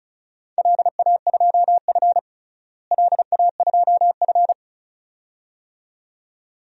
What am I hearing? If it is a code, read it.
Morse code, heard as a keyed tone, about 36 wpm, 700 Hz. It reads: LA2F LA2F
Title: DXCC-100 Call Signs - 4 Characters - Repeated 2x - 35wpm